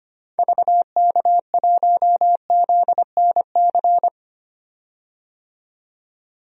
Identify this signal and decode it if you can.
Morse code, heard as a keyed tone, about 25 wpm, 700 Hz. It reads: VK1ZNC
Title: DXCC-100 Call Signs - 6 Characters - 25wpm